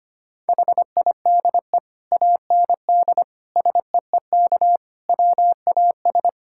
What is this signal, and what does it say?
Morse code, heard as a keyed tone, about 25 wpm, 700 Hz. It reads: HIDE AND SEEK WAS
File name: practice / 25wpm / Difficult Sentences with Hints - 25wpm